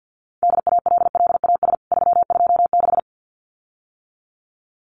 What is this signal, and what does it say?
Morse code, heard as a keyed tone, about 49 wpm, 700 Hz. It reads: DALLAS 326
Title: CWT Exchange 50wpm